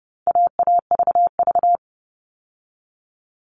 Morse code, heard as a keyed tone, about 30 wpm, 700 Hz. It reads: AAVV
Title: All Cut Numbers - 4-Digit 30wpm